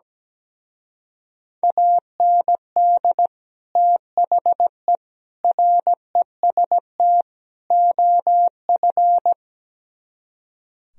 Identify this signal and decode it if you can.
Morse code, heard as a keyed tone, about 17 wpm, 700 Hz. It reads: AND THE REST OF
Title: Speed-Racing - Sets of 4 Words 17wpm